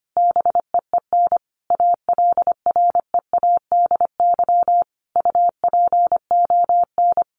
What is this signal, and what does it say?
Morse code, heard as a keyed tone, about 25 wpm, 700 Hz. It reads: BEEN ALREADY UPON
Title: Sentences from Top 1500 Words 25wpm